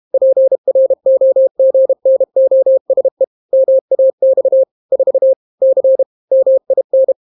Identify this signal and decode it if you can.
Morse code, heard as a keyed tone, about 32 wpm, 530 Hz. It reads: PROGNOSE MAX 4 C MIN